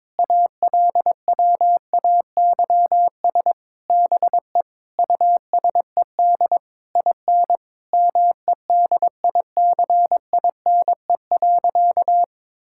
Morse code, heard as a keyed tone, about 22 wpm, 700 Hz. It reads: ALWAYS BE USED IN MEDICINE.
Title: Sentences from Top 1000 Words - Review Part 6 - 22wpm